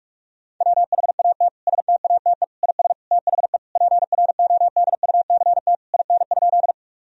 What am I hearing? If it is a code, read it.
Morse code, heard as a keyed tone, about 46 wpm, 700 Hz. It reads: WHAT STATE IS THE PRODUCT IN?
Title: Sentences from Top 1000 Words - Review Part 2 - 45wpm